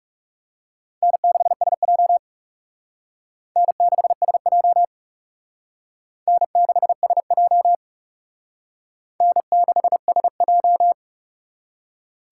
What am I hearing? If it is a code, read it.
Morse code, heard as a keyed tone, about 37 wpm, 700 Hz. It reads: N6SJ N6SJ N6SJ N6SJ
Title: Speed Racing - USA 1x2 Call Signs 30wpm